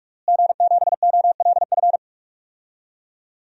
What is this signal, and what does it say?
Morse code, heard as a keyed tone, about 41 wpm, 700 Hz. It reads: G7OLF
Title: DXCC-100 Call Signs - 5 Characters - Repeated 2x - 45wpm